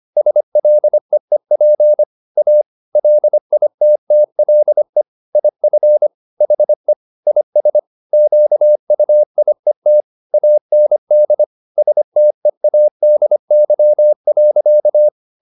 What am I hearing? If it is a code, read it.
Morse code, heard as a keyed tone, about 25 wpm, 600 Hz. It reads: SLEEP A LITTLE IF HE IS QUIET AND STEADY.